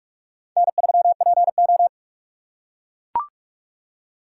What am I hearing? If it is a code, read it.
Morse code, heard as a keyed tone, about 44 wpm, 700 Hz. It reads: N3PO